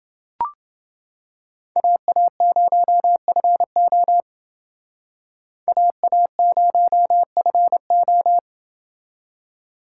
Morse code, heard as a keyed tone, about 29 wpm, 700 Hz. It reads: AA0FO AA0FO